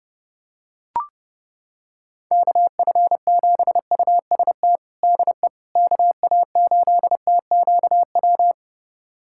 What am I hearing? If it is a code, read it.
Morse code, heard as a keyed tone, about 30 wpm, 700 Hz. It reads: KF7UST DE KA8TQW
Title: USA Callsign DE Callsign 30wpm